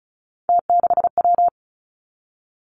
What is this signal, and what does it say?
Morse code, heard as a keyed tone, about 36 wpm, 700 Hz. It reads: T6W
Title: Three Challenging Letter-Number Combinations 35wpm